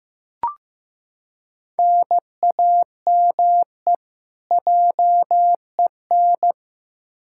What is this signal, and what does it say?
Morse code, heard as a keyed tone, about 16 wpm, 700 Hz. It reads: NAME JEN
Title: K1USN SST QSO Practice 15wpm